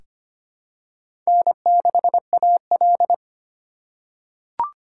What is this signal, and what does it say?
Morse code, heard as a keyed tone, about 25 wpm, 700 Hz. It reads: N6AL
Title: USA Call Signs - 2x Character Spacing - 25wpm